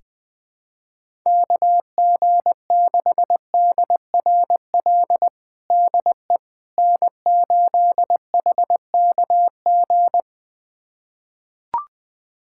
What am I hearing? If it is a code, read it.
Morse code, heard as a keyed tone, about 21 wpm, 700 Hz. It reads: KG6DRL DE N8HKG
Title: USA Callsign DE Callsign 20wpm